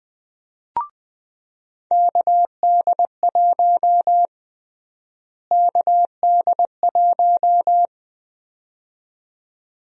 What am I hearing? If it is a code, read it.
Morse code, heard as a keyed tone, about 20 wpm, 700 Hz. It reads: KD1 KD1
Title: USA Call Sign Prefixes - Repeated 2x - 20wpm